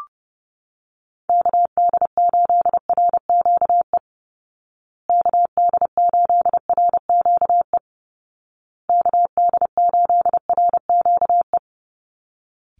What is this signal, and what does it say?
Morse code, heard as a keyed tone, about 30 wpm, 700 Hz. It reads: KD8RQE KD8RQE KD8RQE
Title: Call Signs Repeated 2x - 30wpm